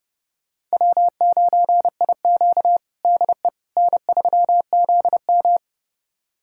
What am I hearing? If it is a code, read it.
Morse code, heard as a keyed tone, about 30 wpm, 700 Hz. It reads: W9IQ DE N3ZM
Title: USA Callsign DE Callsign - 4 Characters - 30wpm